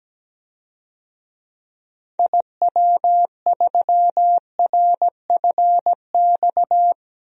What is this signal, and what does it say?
Morse code, heard as a keyed tone, about 17 wpm, 700 Hz. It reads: IW3RFX
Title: DXCC-100 Call Signs - 6 Characters - Repeated 2x - 17wpm